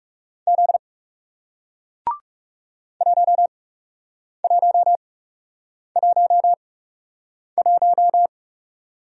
Morse code, heard as a keyed tone, about 37 wpm, spaced as for 8 wpm, 700 Hz. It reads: Z 1111
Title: Speed-Racing Single Letter-Number 30wpm